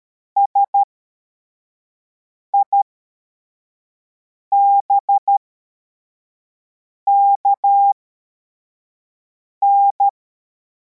Morse code, heard as a keyed tone, about 13 wpm, spaced as for 4 wpm, 800 Hz. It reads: SIBKN
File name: formation / cw / Text53